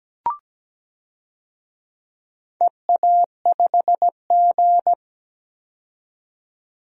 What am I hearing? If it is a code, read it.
Morse code, heard as a keyed tone, about 17 wpm, 700 Hz. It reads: EA5G
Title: DXCC-100 Call Signs - 4 Characters - 17wpm